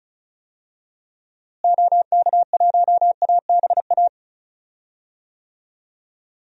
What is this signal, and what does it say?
Morse code, heard as a keyed tone, about 36 wpm, 700 Hz. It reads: OK1ABA
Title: DXCC-100 Call Signs - 6 Characters - Repeated 3x - 35wpm